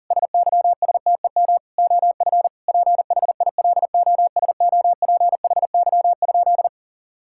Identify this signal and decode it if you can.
Morse code, heard as a keyed tone, about 40 wpm, 700 Hz. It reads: SYSTEM OF PHILOSOPHY?
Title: Sentences from Top 1000 Words - Review Part 6 - 40wpm